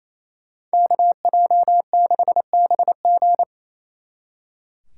Morse code, heard as a keyed tone, about 34 wpm, 700 Hz. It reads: KJ6BG
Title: Speed Racing - USA 2x2 Call Signs 28wpm